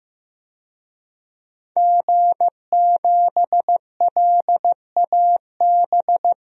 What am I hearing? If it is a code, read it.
Morse code, heard as a keyed tone, about 15 wpm, 700 Hz. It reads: G7LAB
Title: DXCC-100 Call Signs - 5 Characters - 15wpm